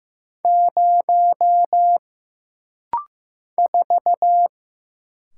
Morse code, heard as a keyed tone, about 15 wpm, 700 Hz. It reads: 0 4
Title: Single Letter-Number Alt - Rapid-Fire 15wpm